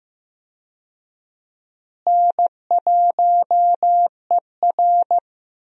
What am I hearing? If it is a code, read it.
Morse code, heard as a keyed tone, about 15 wpm, 700 Hz. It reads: N1ER
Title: USA Call Signs - 4 Characters - Repeated 3x - 15wpm